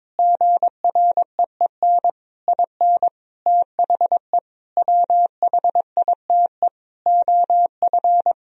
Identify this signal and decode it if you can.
Morse code, heard as a keyed tone, about 22 wpm, 700 Hz. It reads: GREEN IN THE WHITE OF
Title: Sentences from Top 500 Words and Abbreviations 22wpm